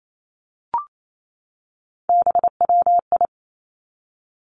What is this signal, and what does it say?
Morse code, heard as a keyed tone, about 28 wpm, 700 Hz. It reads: BWI
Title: Three Challenging Letter-Number Combinations 28wpm